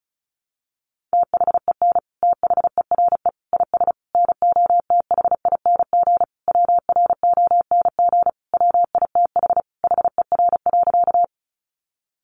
Morse code, heard as a keyed tone, about 34 wpm, 700 Hz. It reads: THEN THERE IS NOTHING WRONG WITH HER.
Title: Sentences from Top 900 Words 35wpm